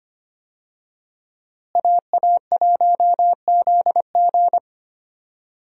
Morse code, heard as a keyed tone, about 25 wpm, 700 Hz. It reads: AA1ZG
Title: USA Call Signs - 5 Characters - Repeated 3x - 25wpm